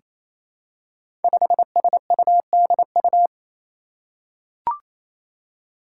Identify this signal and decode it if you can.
Morse code, heard as a keyed tone, about 27 wpm, 700 Hz. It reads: 5SUDU
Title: Five Challenging Letter-Number Combinations 28wpm